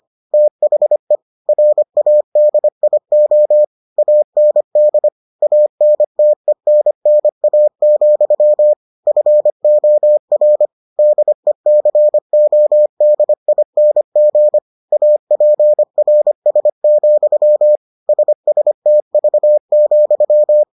Morse code, heard as a keyed tone, about 25 wpm, 600 Hz. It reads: THE RADIO AND ANTENNA, FOR DECODING APRS, SSTV,